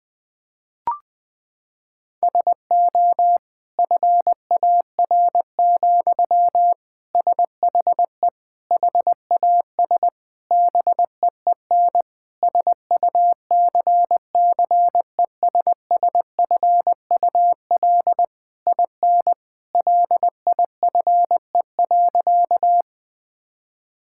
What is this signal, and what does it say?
Morse code, heard as a keyed tone, about 21 wpm, 700 Hz. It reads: SO FAR, SHE HAS BEEN SUCCESSFUL IN LIFE.
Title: Sentences from Top 1000 Words - Review Part 6 - 20wpm